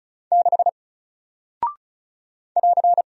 Morse code, heard as a keyed tone, about 34 wpm, 700 Hz. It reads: 6 <AR>
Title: Single Character (Letter:Number:Punctuation:Prosign) Alt 35wpm